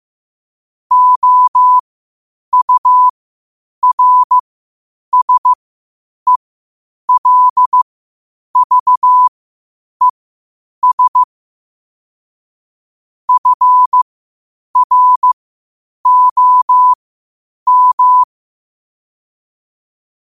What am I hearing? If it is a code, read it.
Morse code, heard as a keyed tone, about 15 wpm, spaced as for 8 wpm, 1000 Hz. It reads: OURSELVES FROM